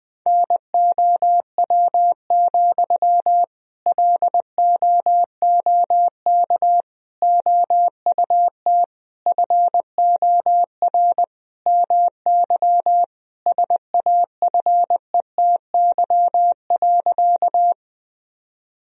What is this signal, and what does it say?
Morse code, heard as a keyed tone, about 19 wpm, 700 Hz. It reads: NOW, LOOK OUT FOR MY SAFETY.
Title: Sentences from Top 1000 Words - Review Part 4 - 20wpm